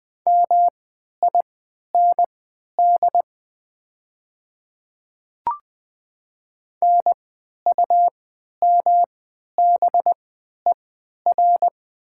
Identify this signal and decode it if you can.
Morse code, heard as a keyed tone, about 20 wpm, spaced as for 11 wpm, 700 Hz. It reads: MIND NUMBER
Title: 3x Character Spacing - Top 500 Words - 20wpm